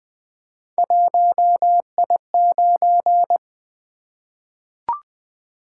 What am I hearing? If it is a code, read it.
Morse code, heard as a keyed tone, about 20 wpm, 700 Hz. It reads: 1I9
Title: Three Letter-Number Combinations - 20wpm